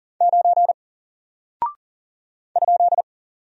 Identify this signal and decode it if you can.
Morse code, heard as a keyed tone, about 40 wpm, 700 Hz. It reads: 9 ?
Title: Single Character (Letter:Number:Punctuation:Prosign) Alt 40wpm